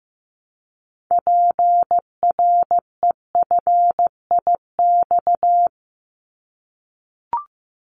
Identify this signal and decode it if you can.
Morse code, heard as a keyed tone, about 15 wpm, 700 Hz. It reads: PREFIX
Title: Callsign Prefixes for Top 25 DXCC - Repeated 3x - 15wpm